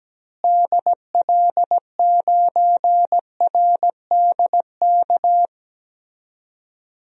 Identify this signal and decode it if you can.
Morse code, heard as a keyed tone, about 17 wpm, 700 Hz. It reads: DL9RDK
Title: DXCC-100 Call Signs - 6 Characters - Repeated 3x - 17wpm